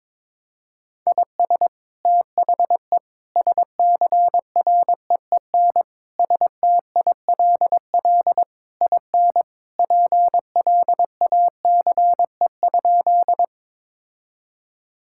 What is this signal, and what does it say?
Morse code, heard as a keyed tone, about 22 wpm, 700 Hz. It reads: IS THE SCREEN STILL IN PLACE?
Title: Sentences from Top 1000 Words - Review Part 6 - 22wpm